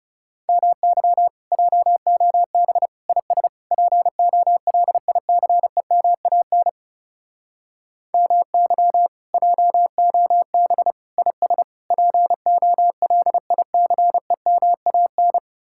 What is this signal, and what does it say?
Morse code, heard as a keyed tone, about 36 wpm, 700 Hz. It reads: MY JOB IS POLICEMAN MY JOB IS POLICEMAN
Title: Speed-Racing Common QSO Phrases - 30wpm